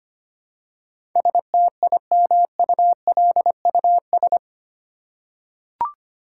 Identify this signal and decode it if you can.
Morse code, heard as a keyed tone, about 25 wpm, 700 Hz. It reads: STIMULUS
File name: 8 Letter Words - First 3 Letters Given - 25wpm